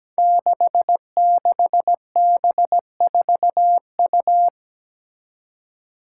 Morse code, heard as a keyed tone, about 17 wpm, 700 Hz. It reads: 66B4U
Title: Five Challenging Letter-Number Combinations 17wpm